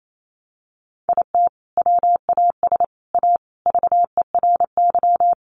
Morse code, heard as a keyed tone, about 28 wpm, 700 Hz. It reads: IT WAS A VERY
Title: Sentences from Top 1400 Words 28wpm